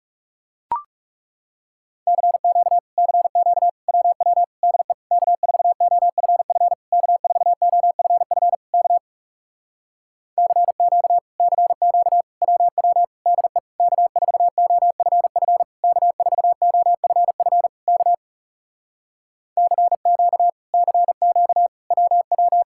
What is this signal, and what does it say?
Morse code, heard as a keyed tone, about 40 wpm, 700 Hz. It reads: CQ CQ WW DE K4OFF K4OFF K CQ CQ WW DE K4OFF K4OFF K CQ CQ WW